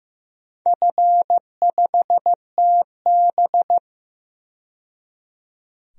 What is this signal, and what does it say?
Morse code, heard as a keyed tone, about 16 wpm, 700 Hz. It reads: F5TB
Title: DXCC-100 Call Signs - 4 Characters - 15wpm